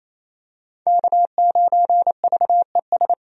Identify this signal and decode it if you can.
Morse code, heard as a keyed tone, about 27 wpm, 700 Hz. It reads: K9VES
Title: USA Call Signs - 5 Characters - 28wpm